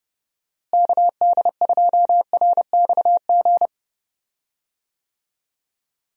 Morse code, heard as a keyed tone, about 30 wpm, 700 Hz. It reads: KD2RXG
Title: USA Call Signs - 6 Characters - Repeated 2x - 30wpm